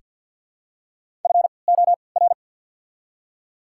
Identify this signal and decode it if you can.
Morse code, heard as a keyed tone, about 48 wpm, spaced as for 28 wpm, 700 Hz. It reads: FOR
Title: 3x Character Spacing - Top 100 Words - 50wpm